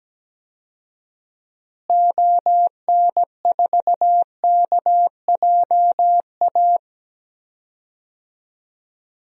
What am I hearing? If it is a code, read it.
Morse code, heard as a keyed tone, about 17 wpm, 700 Hz. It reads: ON4KJA
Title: DXCC-100 Call Signs - 6 Characters - Repeated 2x - 17wpm